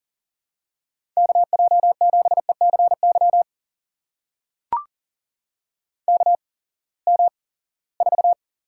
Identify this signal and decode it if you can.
Morse code, heard as a keyed tone, about 40 wpm, spaced as for 23 wpm, 700 Hz. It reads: KJ7ECY KM4
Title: USA Call Signs - 6 Characters - 8x Character Spacing - 40wpm